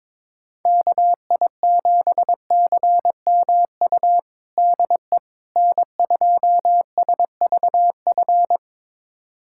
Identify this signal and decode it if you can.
Morse code, heard as a keyed tone, about 22 wpm, 700 Hz. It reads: KI7CMU DE N2SVF
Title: USA Callsign DE Callsign 22wpm